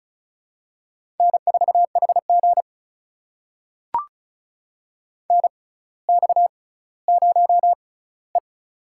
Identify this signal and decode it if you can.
Morse code, heard as a keyed tone, about 36 wpm, spaced as for 18 wpm, 700 Hz. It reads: N4HG NX0E
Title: USA Call Signs - 4 Characters - 6x Character Spacing - 35wpm